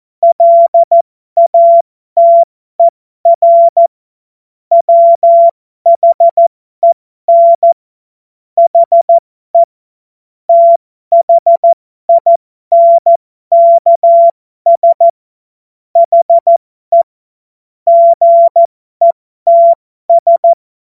Morse code, heard as a keyed tone, about 14 wpm, spaced as for 12 wpm, 680 Hz. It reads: LATER WHEN HE THINKS HE GETS
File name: Morsekurs / Witze